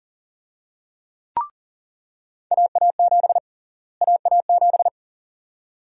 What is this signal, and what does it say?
Morse code, heard as a keyed tone, about 40 wpm, 700 Hz. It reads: AA7 AA7
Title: USA Call Sign Prefixes - Repeated 2x - 40wpm